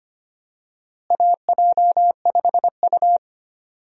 Morse code, heard as a keyed tone, about 25 wpm, 700 Hz. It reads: AJ5U
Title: USA Call Signs - 4 Characters - Repeated 3x - 25wpm